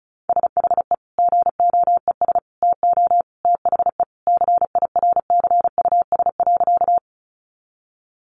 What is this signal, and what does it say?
Morse code, heard as a keyed tone, about 34 wpm, 700 Hz. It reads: SHE GOES TO THE CIRCUS.
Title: Easy Phrases 35wpm